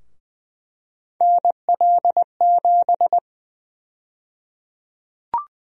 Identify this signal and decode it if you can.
Morse code, heard as a keyed tone, about 21 wpm, 700 Hz. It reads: NL7
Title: USA Call Sign Prefixes - 20wpm